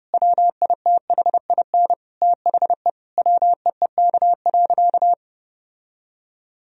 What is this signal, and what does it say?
Morse code, heard as a keyed tone, about 30 wpm, 700 Hz. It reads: WITHIN THE WEEK.
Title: Sentences from Top 1000 Words - Review Part 4 - 30wpm